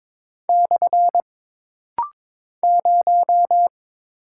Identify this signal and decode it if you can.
Morse code, heard as a keyed tone, about 22 wpm, 700 Hz. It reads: / 0
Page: This rapid-fire timing is geared to push your ICR (Instant Character Recognition) skills.